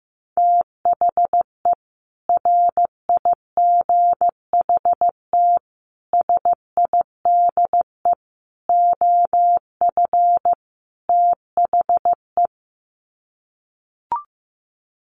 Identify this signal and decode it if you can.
Morse code, heard as a keyed tone, about 15 wpm, 700 Hz. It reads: THE RIGHT SIDE OF THE
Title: Sets of 5 Words Encore 15wpm